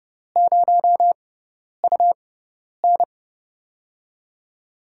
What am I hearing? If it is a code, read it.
Morse code, heard as a keyed tone, about 31 wpm, spaced as for 10 wpm, 700 Hz. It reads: 0UN
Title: USA Call Signs - 4 Characters - 6x Character Spacing - 30wpm